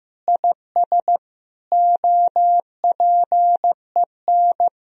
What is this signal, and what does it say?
Morse code, heard as a keyed tone, about 15 wpm, 700 Hz. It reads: IS OPEN
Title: Sentences from Top 1000 Words - Review Part 4 - 15wpm